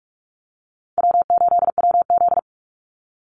Morse code, heard as a keyed tone, about 44 wpm, 700 Hz. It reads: W8WZ
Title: Easy Call Signs Repeated 3x - 45wpm